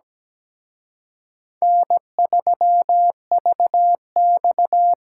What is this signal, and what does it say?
Morse code, heard as a keyed tone, about 20 wpm, 700 Hz. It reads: N3VX
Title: Speed Racing - USA 1x2 Call Signs 17wpm